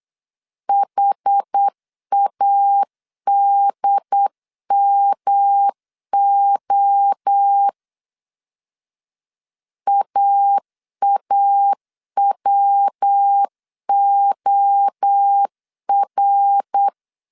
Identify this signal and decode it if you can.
Morse code, heard as a keyed tone, about 8 wpm, 800 Hz. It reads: HADMO AAWOR
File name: CW / DET / 1 3 1500C S A 8wpm 4229